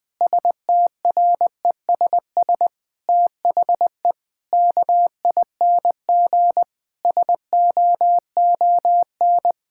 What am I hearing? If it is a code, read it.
Morse code, heard as a keyed tone, about 21 wpm, 700 Hz. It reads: STRESS THE KING SOON
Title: Sentences from Top 1000 Words - Review Part 6 - 20wpm